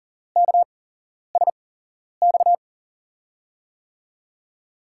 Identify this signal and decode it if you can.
Morse code, heard as a keyed tone, about 40 wpm, spaced as for 11 wpm, 700 Hz. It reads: KSX
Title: USA Call Signs - 6 Characters - 8x Character Spacing - 40wpm